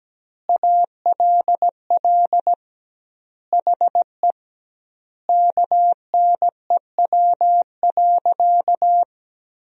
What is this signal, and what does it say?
Morse code, heard as a keyed tone, about 17 wpm, 700 Hz. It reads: ALL HE KNEW.
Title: 2x Word-Spacing - Sentences from Top 400 Words - 17wpm